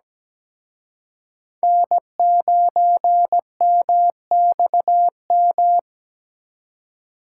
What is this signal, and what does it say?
Morse code, heard as a keyed tone, about 20 wpm, 700 Hz. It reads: N9MXM
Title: Speed Racing - USA 1x3 Call Signs 17wpm